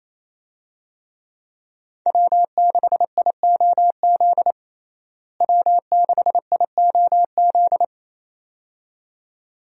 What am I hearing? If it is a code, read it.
Morse code, heard as a keyed tone, about 28 wpm, 700 Hz. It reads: W6IOZ W6IOZ
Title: USA Call Signs - 5 Characters - Repeated 2x - 28wpm